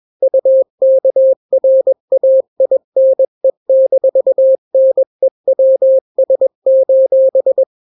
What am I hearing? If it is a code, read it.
Morse code, heard as a keyed tone, about 21 wpm, 530 Hz. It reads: UKRAINE-NEWS:
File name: daily / 20wpm